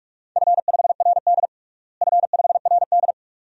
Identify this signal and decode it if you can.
Morse code, heard as a keyed tone, about 47 wpm, 700 Hz. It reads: F5RD F5RD
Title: DXCC-100 Call Signs - 4 Characters - Repeated 2x - 45wpm